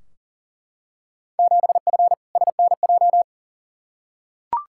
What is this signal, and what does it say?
Morse code, heard as a keyed tone, about 40 wpm, 700 Hz. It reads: 7F SNJ
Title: Field Day Exchanges Encore - 40wpm